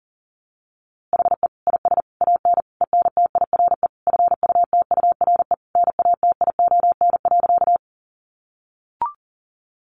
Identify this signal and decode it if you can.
Morse code, heard as a keyed tone, about 40 wpm, 700 Hz. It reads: HE IS AN ENTIRE FUTURE NATION.
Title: A Sound Of Thunder 40wpm